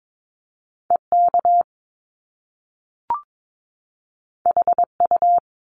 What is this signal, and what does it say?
Morse code, heard as a keyed tone, about 22 wpm, 700 Hz. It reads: EK HU
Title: Top Two Letter Combinations in Words 22wpm